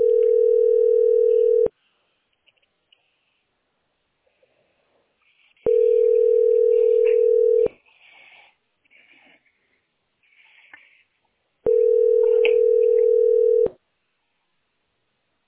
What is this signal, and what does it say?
Ringback tone (North America), 440 + 480 Hz.